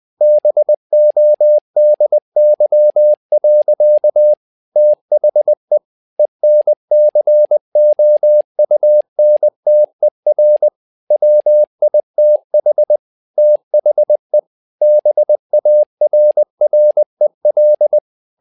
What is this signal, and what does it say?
Morse code, heard as a keyed tone, about 20 wpm, 600 Hz. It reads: BODY. THE ENCOUNTER WITH THE BARREL
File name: learning center / tower trouble / tower trouble 20wpm